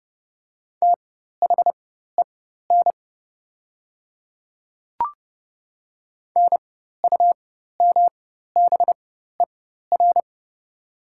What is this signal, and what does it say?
Morse code, heard as a keyed tone, about 30 wpm, spaced as for 14 wpm, 700 Hz. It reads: THEN NUMBER
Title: 4x Character Spacing - Top 100 Words - 30wpm